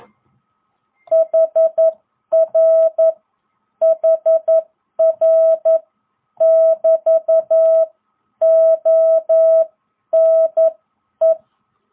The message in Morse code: HR HR<BT>ONE

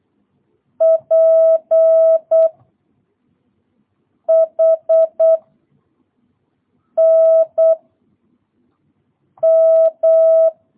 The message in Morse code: P H N M

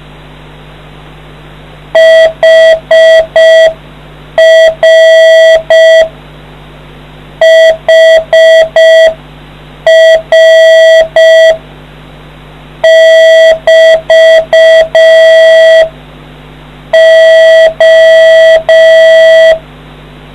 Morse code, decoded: HR HR<BT> O

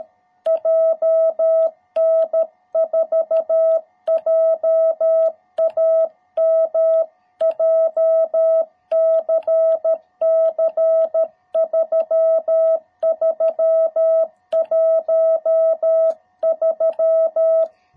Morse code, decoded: JN4JAM JCC 3313

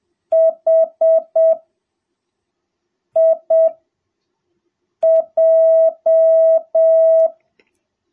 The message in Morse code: HIJ

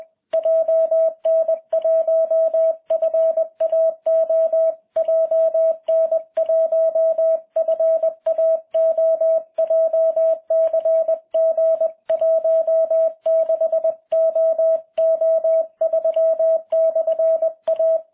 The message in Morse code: JN1FAO JN1FAO JCG 16OO3/A

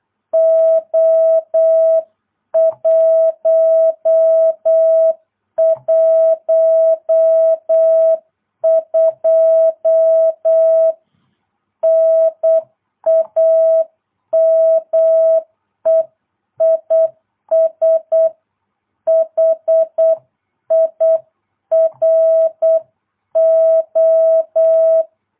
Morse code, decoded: O112 NAME IS HIRO

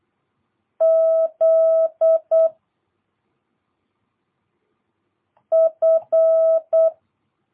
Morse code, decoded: Z F

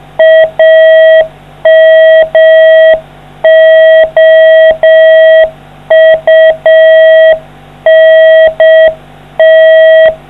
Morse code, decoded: AMOUNT

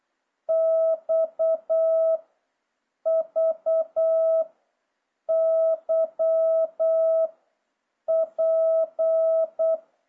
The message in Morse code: X V Y P

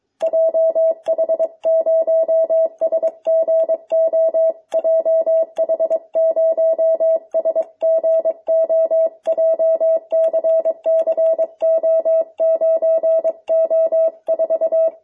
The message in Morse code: JH0SGO JH0SGO JCCO9O4

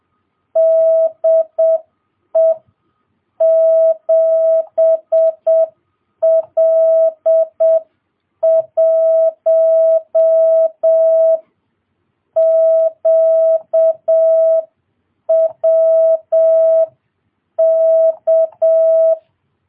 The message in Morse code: DE 7L1QWK

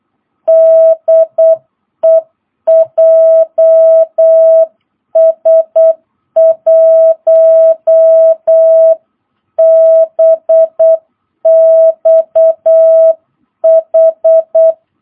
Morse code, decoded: DE JS1BXH

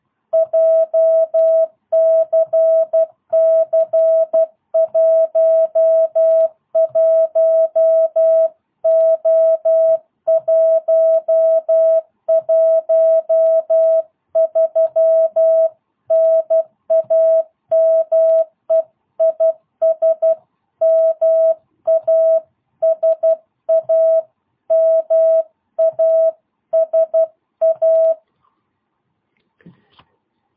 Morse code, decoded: JCC11O113 NAME IS MASA MASA

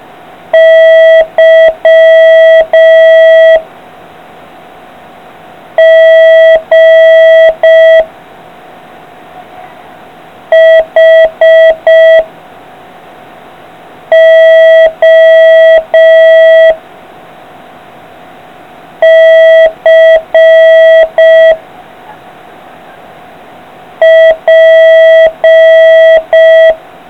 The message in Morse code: Y G H O C P

